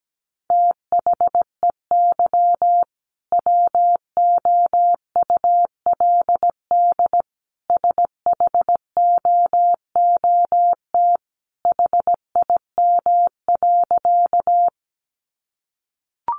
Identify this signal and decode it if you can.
Morse code, heard as a keyed tone, about 17 wpm, 700 Hz. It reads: THEY WOULD SHOOT HIM.
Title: Sentences from Top 1200 Words 17wpm